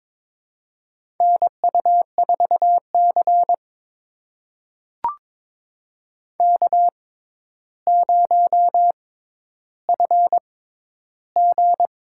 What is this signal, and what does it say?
Morse code, heard as a keyed tone, about 22 wpm, spaced as for 11 wpm, 700 Hz. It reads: NU4C K0FG
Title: USA Call Signs - 4 Characters - 6x Character Spacing - 22wpm